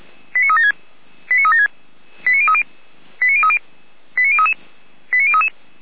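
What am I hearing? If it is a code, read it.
Five-tone selective call (ZVEI), heard as quick runs of five tones, tones of about 70 ms. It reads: 89376 89376 89030 89030 89033 89033
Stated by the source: ZVEI five-tone call received by SDR.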